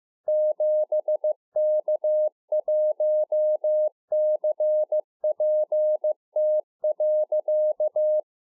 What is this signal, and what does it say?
Morse code, heard as a keyed tone, about 15 wpm, 610 Hz. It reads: 7K1CPT.